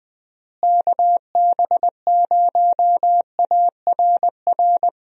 Morse code, heard as a keyed tone, about 20 wpm, 700 Hz. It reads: KB0ARR
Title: USA Call Signs - 6 Characters - Repeated 3x - 20wpm